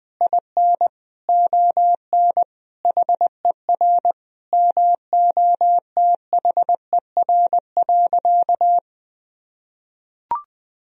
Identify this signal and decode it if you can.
Morse code, heard as a keyed tone, about 20 wpm, 700 Hz. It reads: IN ON HER MOTHER.
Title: Sentences from Top 1000 Words - Review Part 3 - 20wpm